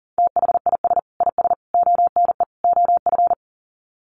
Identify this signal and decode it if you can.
Morse code, heard as a keyed tone, about 40 wpm, 700 Hz. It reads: THIS IS ONE OF
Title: Sets of 4 Words 40wpm